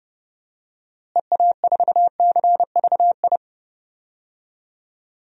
Morse code, heard as a keyed tone, about 32 wpm, 700 Hz. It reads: EA4CVI
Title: DXCC-100 Call Signs - 6 Characters - Repeated 2x - 30wpm